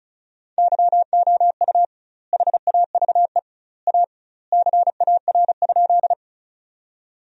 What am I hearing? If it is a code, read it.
Morse code, heard as a keyed tone, about 36 wpm, 700 Hz. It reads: YOU HAVE A CAR?
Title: 2x Word-Spacing - Sentences from Top 300 Words - 35wpm